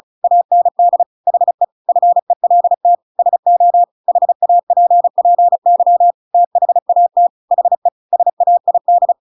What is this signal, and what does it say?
Morse code, heard as a keyed tone, about 34 wpm, 700 Hz. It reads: AND HE FELT SO HAPPY THAT HE SAID
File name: Pinocchio / Pinocchio ch8 35 35 ik1hge